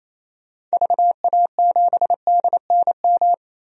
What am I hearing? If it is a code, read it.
Morse code, heard as a keyed tone, about 28 wpm, 700 Hz. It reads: VA7DNM
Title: DXCC-100 Call Signs - 6 Characters - 28wpm